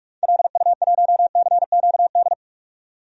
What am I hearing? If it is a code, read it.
Morse code, heard as a keyed tone, about 46 wpm, 700 Hz. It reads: LU1CQD
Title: DXCC-100 Call Signs - 6 Characters - 45wpm